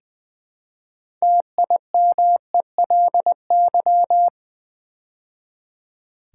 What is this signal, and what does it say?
Morse code, heard as a keyed tone, about 20 wpm, 700 Hz. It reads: TIMELY
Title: Build A Word - 2 Letters At A Time - 6-Letter Words - 20wpm